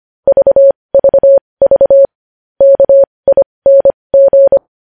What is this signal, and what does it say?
Morse code, heard as a keyed tone, about 25 wpm, spaced as for 20 wpm, 550 Hz. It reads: VVV KING
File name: podcast / News headlines-1x-25x20